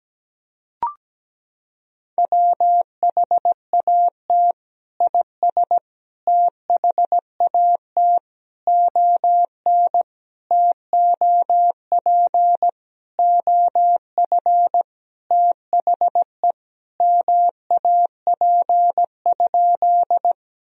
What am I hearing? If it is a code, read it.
Morse code, heard as a keyed tone, about 17 wpm, 700 Hz. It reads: WHAT IS THAT ON TOP OF THE MAP?
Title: Sentences from Top 1000 Words - Review Part 2 - 17wpm